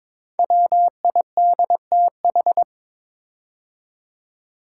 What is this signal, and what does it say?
Morse code, heard as a keyed tone, about 22 wpm, 700 Hz. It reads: WIDTH
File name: 5 Letter Words - First 3 Letters Given - 22wpm